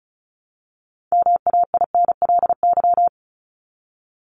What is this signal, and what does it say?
Morse code, heard as a keyed tone, about 34 wpm, 700 Hz. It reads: MAINLY
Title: Top 1251-1500 Words 35wpm